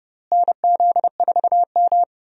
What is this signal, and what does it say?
Morse code, heard as a keyed tone, about 30 wpm, 700 Hz. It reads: NZ4M
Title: USA Call Signs - 4 Characters - Repeated 2x - 30wpm